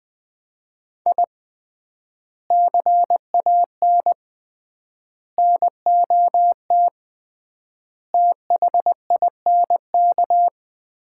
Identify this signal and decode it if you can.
Morse code, heard as a keyed tone, about 20 wpm, 700 Hz. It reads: I CAN NOT THINK
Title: 3x Word-Spacing - Sets of 5 Words - 20wpm